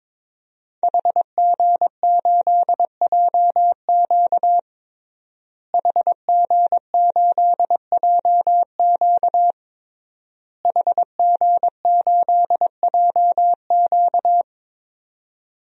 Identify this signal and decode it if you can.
Morse code, heard as a keyed tone, about 22 wpm, 700 Hz. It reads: HG8JQ HG8JQ HG8JQ
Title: DXCC-100 Call Signs - 5 Characters - Repeated 3x - 22wpm